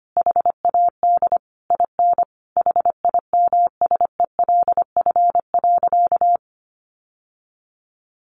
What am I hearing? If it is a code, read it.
Morse code, heard as a keyed tone, about 25 wpm, 700 Hz. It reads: HAD IN HIMSELF.
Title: Sentences from Top 1500 Words 25wpm